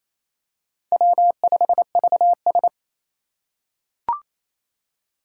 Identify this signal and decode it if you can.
Morse code, heard as a keyed tone, about 28 wpm, 700 Hz. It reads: W5VS
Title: USA Call Signs - 4 Characters - 2x Character Spacing - 28wpm